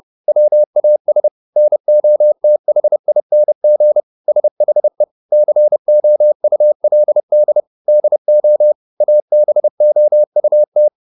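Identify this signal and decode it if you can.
Morse code, heard as a keyed tone, about 30 wpm, 600 Hz. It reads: WAS NOTHING SHE COULD DO ABOUT
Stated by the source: Morse code at 30 wpm, the novel Never by Ken Follett in English, from CW med Gnister.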